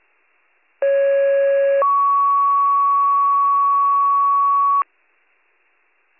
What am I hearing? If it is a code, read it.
Two-tone page, heard as a one-second tone, then a three-second tone, about 550 then 1120 Hz. It is heard from 0.8 to 4.8 s.